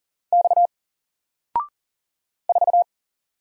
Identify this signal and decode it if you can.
Morse code, heard as a keyed tone, about 41 wpm, 700 Hz. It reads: X 4
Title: Single Letter-Number Alt 40wpm